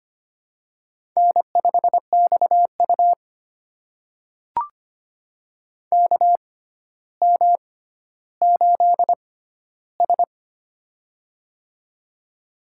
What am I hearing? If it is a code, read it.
Morse code, heard as a keyed tone, about 25 wpm, spaced as for 13 wpm, 700 Hz. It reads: N5XU KM8S
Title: USA Call Signs - 4 Characters - 6x Character Spacing - 25wpm